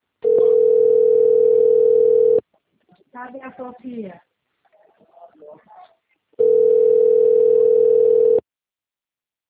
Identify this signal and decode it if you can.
Ringback tone (North America), 440 + 480 Hz, with speech.
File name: rg-1600-unknown- - - .16817